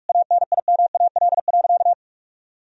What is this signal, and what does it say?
Morse code, heard as a keyed tone, about 46 wpm, 700 Hz. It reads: ANIMAL.
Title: Sentences from Top 1000 Words - Review Part 2 - 45wpm